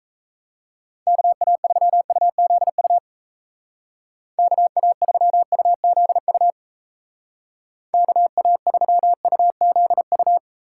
Morse code, heard as a keyed tone, about 35 wpm, 700 Hz. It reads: KA3UZU KA3UZU KA3UZU